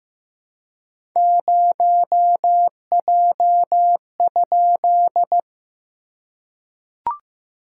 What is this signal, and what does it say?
Morse code, heard as a keyed tone, about 15 wpm, 700 Hz. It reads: 0J?
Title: SOTA QSO Practice 15wpm